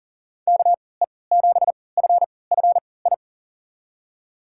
Morse code, heard as a keyed tone, about 40 wpm, spaced as for 23 wpm, 700 Hz. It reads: KE7FFI
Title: USA Call Signs - 3x Character Spacing - 40wpm